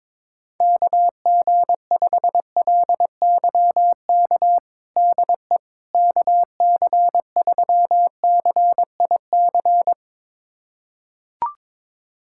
Morse code, heard as a keyed tone, about 22 wpm, 700 Hz. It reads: KG5LYK DE KC3CIC
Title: USA Callsign DE Callsign - 6 Characters - 22wpm